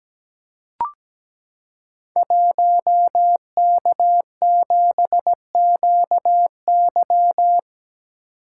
Morse code, heard as a keyed tone, about 17 wpm, 700 Hz. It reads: 1K7QY
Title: Code Groups - 5 Letter:Numbers - 17wpm